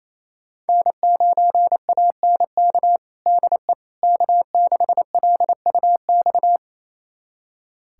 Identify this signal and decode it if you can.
Morse code, heard as a keyed tone, about 27 wpm, 700 Hz. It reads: N9ANK DE K6LUX
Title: USA Callsign DE Callsign 28wpm